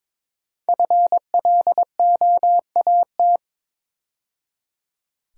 Morse code, heard as a keyed tone, about 23 wpm, 700 Hz. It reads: FLOAT
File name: 5 Letter Words - First 3 Letters Given - 22wpm